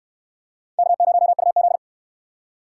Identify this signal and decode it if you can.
Morse code, heard as a keyed tone, about 68 wpm, 700 Hz. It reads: K0UZ